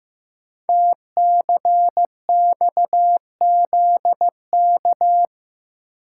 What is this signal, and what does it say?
Morse code, heard as a keyed tone, about 15 wpm, 700 Hz. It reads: TCXZK
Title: Code Groups - 5 Letters - 15wpm